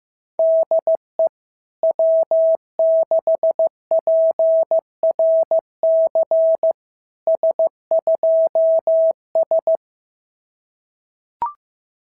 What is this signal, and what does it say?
Morse code, heard as a keyed tone, about 15 wpm, 650 Hz. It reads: DE W6PRC S2S
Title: SOTA QSO Practice 15wpm